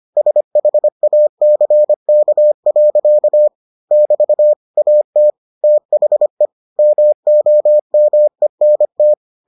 Morse code, heard as a keyed tone, about 25 wpm, 600 Hz. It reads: SHACK. <BT> AT THE MOMENT